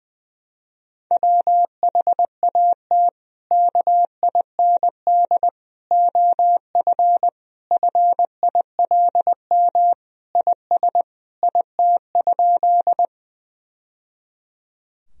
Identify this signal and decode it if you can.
Morse code, heard as a keyed tone, about 21 wpm, 700 Hz. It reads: WHAT KIND OF FILM IS IT?
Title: Sentences from Top 1000 Words - Review Part 3 - 20wpm